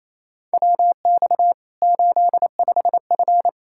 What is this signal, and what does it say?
Morse code, heard as a keyed tone, about 28 wpm, 700 Hz. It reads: WX 85F
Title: Practice QSO - Exchange 2 - Info - 28wpm